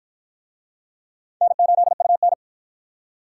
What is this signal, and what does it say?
Morse code, heard as a keyed tone, about 50 wpm, 700 Hz. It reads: N8UN